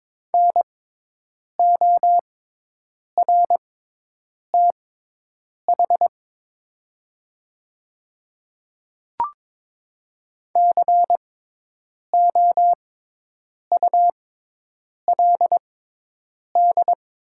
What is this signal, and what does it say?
Morse code, heard as a keyed tone, about 22 wpm, spaced as for 8 wpm, 700 Hz. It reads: NORTH COULD
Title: 6x Character Spacing - Top 300 Words - 22wpm